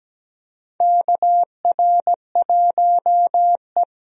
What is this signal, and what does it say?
Morse code, heard as a keyed tone, about 17 wpm, 700 Hz. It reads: KR1E
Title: USA Call Signs - 4 Characters - Repeated 2x - 17wpm